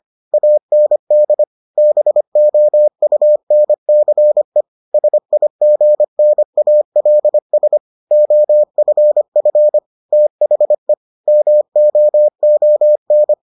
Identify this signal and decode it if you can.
Morse code, heard as a keyed tone, about 25 wpm, 600 Hz. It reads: AND BOUNCE SIGNALS OFF THE MOON